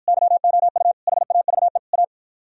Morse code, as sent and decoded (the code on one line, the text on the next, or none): -.-- --- ..- / .... .- ...- . / .-
YOU HAVE A